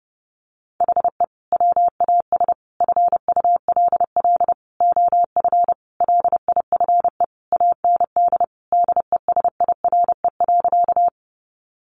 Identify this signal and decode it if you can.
Morse code, heard as a keyed tone, about 30 wpm, 700 Hz. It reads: HE WAS FULL OF LIFE AND DESIRE.
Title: Sentences from Top 1100 Words 30wpm